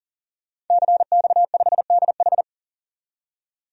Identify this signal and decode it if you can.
Morse code, heard as a keyed tone, about 40 wpm, 700 Hz. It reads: CX5DH
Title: DXCC-100 Call Signs - 5 Characters - 40wpm